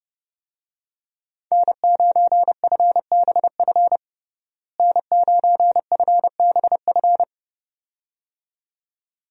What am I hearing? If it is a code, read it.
Morse code, heard as a keyed tone, about 30 wpm, 700 Hz. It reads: N9FBF N9FBF
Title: USA Call Signs - 5 Characters - Repeated 2x - 30wpm